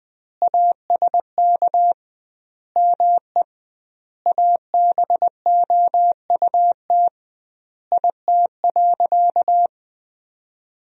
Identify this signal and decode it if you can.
Morse code, heard as a keyed tone, about 21 wpm, 700 Hz. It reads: ASK ME ABOUT IT.
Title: 2x Word-Spacing - Sentences from Top 200 Words - 20wpm